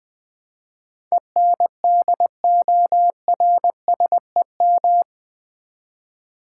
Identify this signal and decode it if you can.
Morse code, heard as a keyed tone, about 20 wpm, 700 Hz. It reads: ENDORSEM
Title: Build A Word - 4 Letters At A Time - 11-Letter Words - 20wpm